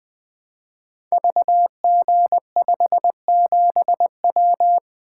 Morse code, heard as a keyed tone, about 20 wpm, 700 Hz. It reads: VG57W
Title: Code Groups - 5 Letter:Numbers - 20wpm